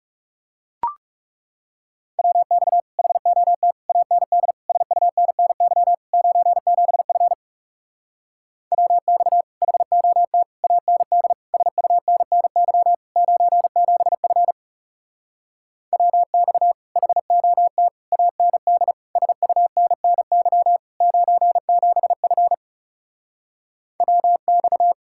Morse code, heard as a keyed tone, about 40 wpm, 700 Hz. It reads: WX HOT AND SUNNY 97F WX HOT AND SUNNY 97F WX HOT AND SUNNY 97F WX